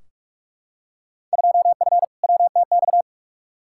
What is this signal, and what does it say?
Morse code, heard as a keyed tone, about 46 wpm, 700 Hz. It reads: 2F WTX
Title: Field Day Exchanges - 45wpm